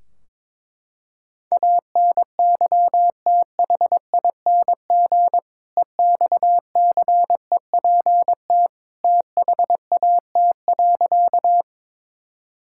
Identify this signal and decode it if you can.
Morse code, heard as a keyed tone, about 21 wpm, 700 Hz. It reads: ANYTHING EXCEPT THAT.
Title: Sentences from Top 1600 Words 22wpm